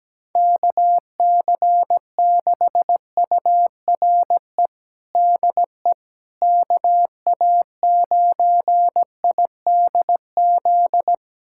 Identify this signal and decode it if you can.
Morse code, heard as a keyed tone, about 17 wpm, 700 Hz. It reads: KC6URE DE KA9IDZ
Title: USA Callsign DE Callsign - 6 Characters - 17wpm